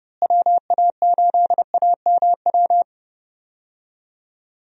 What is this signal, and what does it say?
Morse code, heard as a keyed tone, about 32 wpm, 700 Hz. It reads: WA8AMW
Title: USA Call Signs - 6 Characters - Repeated 3x - 30wpm